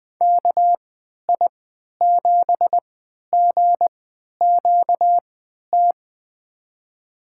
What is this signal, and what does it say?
Morse code, heard as a keyed tone, about 21 wpm, spaced as for 11 wpm, 700 Hz. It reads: KI7GQT
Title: USA Call Signs - 6 Characters - 3x Character Spacing - 20wpm